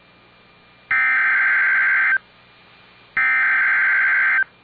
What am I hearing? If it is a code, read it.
Emergency Alert System (SAME) tones (header bursts), heard as harsh buzzing bursts. It is heard from 0.9 to 4.4 s.